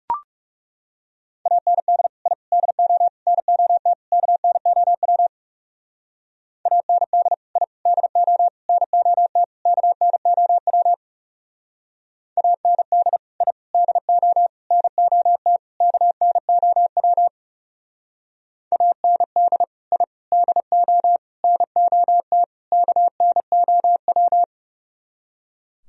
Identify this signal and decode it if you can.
Morse code, heard as a keyed tone, about 34 wpm, 700 Hz. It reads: AND I DO NOT KNOW AND I DO NOT KNOW AND I DO NOT KNOW AND I DO NOT KNOW
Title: Speed-Racing - Sets of 5 Words 30wpm